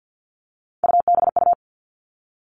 Morse code, heard as a keyed tone, about 50 wpm, 700 Hz. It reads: UBU
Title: Three Challenging Letter-Number Combinations 50wpm